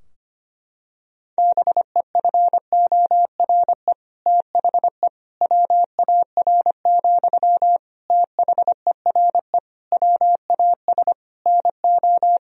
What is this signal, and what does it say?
Morse code, heard as a keyed tone, about 25 wpm, 700 Hz. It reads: BEFORE THE WAR, THERE WAS NO
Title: Sentences from Top 1000 Words - Review Part 2 - 25wpm